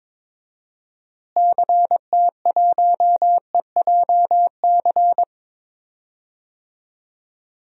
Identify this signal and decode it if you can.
Morse code, heard as a keyed tone, about 22 wpm, 700 Hz. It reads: CT1EJC
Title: DXCC-100 Call Signs - 6 Characters - Repeated 2x - 22wpm